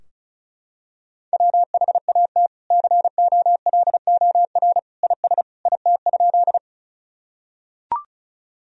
Morse code, heard as a keyed tone, about 34 wpm, 700 Hz. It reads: WHAT COLOR IS IT?
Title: Sentences from Top 1000 Words - Review Part 2 - 35wpm